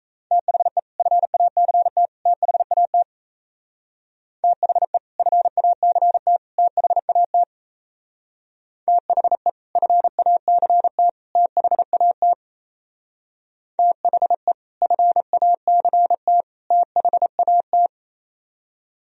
Morse code, heard as a keyed tone, about 35 wpm, 700 Hz. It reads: THE FACT THAT THE FACT THAT THE FACT THAT THE FACT THAT